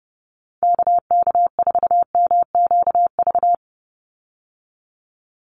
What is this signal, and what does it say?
Morse code, heard as a keyed tone, about 30 wpm, 700 Hz. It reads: KK4MQV
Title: USA Call Signs 30wpm